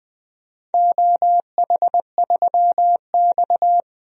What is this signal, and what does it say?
Morse code, heard as a keyed tone, about 21 wpm, 700 Hz. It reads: OH3X
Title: DXCC-100 Call Signs - 4 Characters - 20wpm